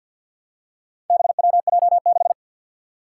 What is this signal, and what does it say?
Morse code, heard as a keyed tone, about 51 wpm, 700 Hz. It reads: BWJ6
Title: Four Challenging Letter-Number Combinations 50wpm